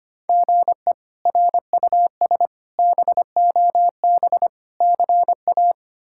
Morse code, heard as a keyed tone, about 25 wpm, 700 Hz. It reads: GE RUS BOB CA
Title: K1USN SST QSO Practice - 25wpm